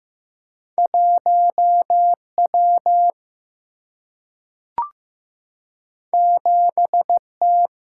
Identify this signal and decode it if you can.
Morse code, heard as a keyed tone, about 15 wpm, 700 Hz. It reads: 1W 7T
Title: Single-Letter Single-Number 15wpm